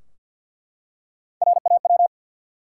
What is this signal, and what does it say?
Morse code, heard as a keyed tone, about 47 wpm, 700 Hz. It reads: RAW
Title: Three Letter Words - 50wpm